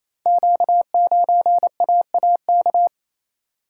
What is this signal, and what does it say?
Morse code, heard as a keyed tone, about 27 wpm, 700 Hz. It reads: Q9AAK
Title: Code Groups - 5 Letter:Numbers - 28wpm